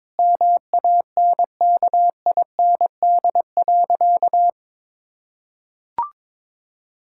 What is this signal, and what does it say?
Morse code, heard as a keyed tone, about 22 wpm, 700 Hz. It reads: MANKIND.
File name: Sentences with 9-Word Phrases - 22wpm